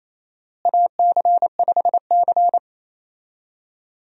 Morse code, heard as a keyed tone, about 28 wpm, 700 Hz. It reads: AC5C
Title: USA Call Signs - 4 Characters - Repeated 2x - 28wpm